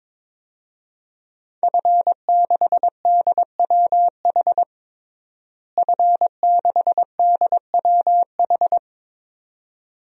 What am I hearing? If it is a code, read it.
Morse code, heard as a keyed tone, about 22 wpm, 700 Hz. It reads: F6DWH F6DWH
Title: DXCC-100 Call Signs - 5 Characters - Repeated 2x - 22wpm